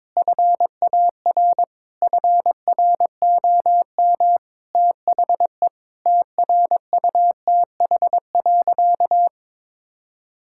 Morse code, heard as a keyed tone, about 22 wpm, 700 Hz. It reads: FAR FROM THE TRUTH.
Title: Sentences from Top 1600 Words 22wpm